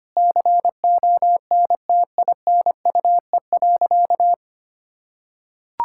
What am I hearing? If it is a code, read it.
Morse code, heard as a keyed tone, about 25 wpm, 700 Hz. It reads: CONTINUE.
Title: Sentences from Top 1000 Words - Review Part 6 - 25wpm